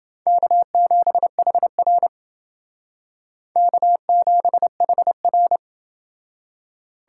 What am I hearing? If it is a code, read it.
Morse code, heard as a keyed tone, about 28 wpm, 700 Hz. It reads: K7HR K7HR